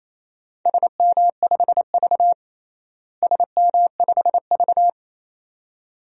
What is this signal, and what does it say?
Morse code, heard as a keyed tone, about 28 wpm, 700 Hz. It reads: SM5V SM5V
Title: DXCC-100 Call Signs - 4 Characters - Repeated 2x - 28wpm